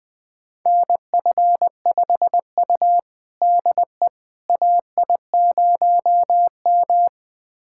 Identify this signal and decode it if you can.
Morse code, heard as a keyed tone, about 20 wpm, 700 Hz. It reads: NF5U DE AI0M
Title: USA Callsign DE Callsign - 4 Characters - 20wpm